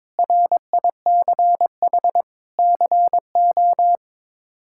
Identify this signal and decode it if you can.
Morse code, heard as a keyed tone, about 22 wpm, 700 Hz. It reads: RICH CO
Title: K1USN SST QSO Practice - 22wpm